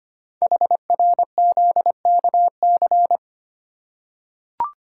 Morse code, heard as a keyed tone, about 25 wpm, 700 Hz. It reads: HRZKC
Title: Code Groups - 5 Letters - 25wpm